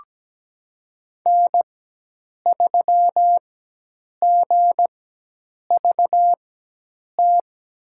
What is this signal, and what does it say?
Morse code, heard as a keyed tone, about 17 wpm, spaced as for 8 wpm, 700 Hz. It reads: N3GVT
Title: USA Call Signs - 5 Characters - 4x Character Spacing - 17wpm